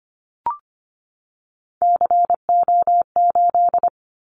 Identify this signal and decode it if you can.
Morse code, heard as a keyed tone, about 25 wpm, 700 Hz. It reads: CO8
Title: Callsign Prefixes for Top 100 DXCC - 25wpm